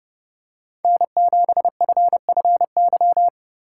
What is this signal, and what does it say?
Morse code, heard as a keyed tone, about 30 wpm, 700 Hz. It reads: N7FFY
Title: USA Call Signs - 5 Characters - 4x Character Spacing - 30wpm